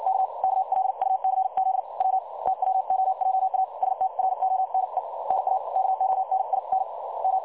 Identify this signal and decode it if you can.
Morse code, heard as a keyed tone, about 56 wpm, spaced as for 49 wpm, 760 Hz. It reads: K KLUCZ W TWOJA STRONE NA KONIN M
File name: dlazaawansowanych / nagraniazpasma / websdr recording 3565.0kHz